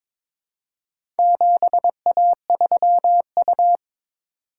Morse code, heard as a keyed tone, about 22 wpm, 700 Hz. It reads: 7A3U
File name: practice / 22wpm / Four Letter-Number Combinations - 22wpm